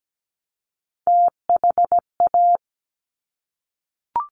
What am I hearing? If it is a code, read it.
Morse code, heard as a keyed tone, about 17 wpm, 700 Hz. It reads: THA
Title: Top Three Letter Combinations in Words 17wpm